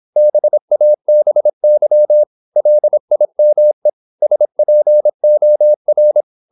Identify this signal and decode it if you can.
Morse code, heard as a keyed tone, about 26 wpm, 600 Hz. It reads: BABY LIME SPOR
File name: Korte ord DK 26wpm